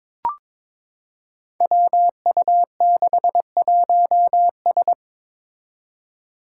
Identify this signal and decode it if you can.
Morse code, heard as a keyed tone, about 22 wpm, 700 Hz. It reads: WU61S
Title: Five Challenging Letter-Number Combinations 22wpm